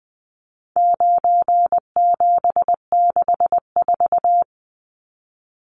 Morse code, heard as a keyed tone, about 20 wpm, 700 Hz. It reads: 9764
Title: Numbers - 4-Digit 20wpm